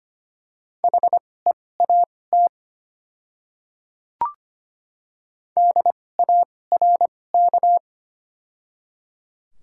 Morse code, heard as a keyed tone, about 25 wpm, spaced as for 18 wpm, 700 Hz. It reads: HEAT DARK
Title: 2x Character Spacing - Top 500 Words - 25wpm